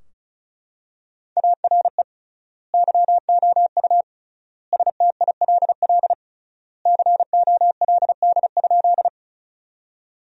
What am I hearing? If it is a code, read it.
Morse code, heard as a keyed tone, about 34 wpm, 700 Hz. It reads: ARE YOU STILL COLD?
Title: 3x Word-Spacing - Sentences from Top 400 Words - 35wpm